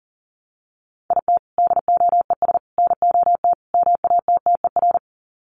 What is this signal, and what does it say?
Morse code, heard as a keyed tone, about 40 wpm, 700 Hz. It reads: IT DOES NOT MATTER
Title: Sets of 4 Words 40wpm